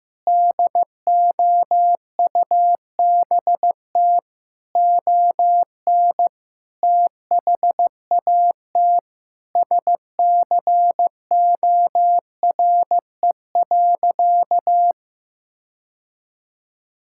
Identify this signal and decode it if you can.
Morse code, heard as a keyed tone, about 15 wpm, 700 Hz. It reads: DOUBT ON THAT SCORE.
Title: Sentences from Top 1600 Words 15wpm